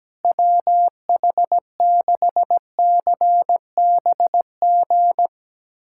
Morse code, heard as a keyed tone, about 17 wpm, 700 Hz. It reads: WH6CBG
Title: USA Callsign DE Callsign - 6 Characters - 17wpm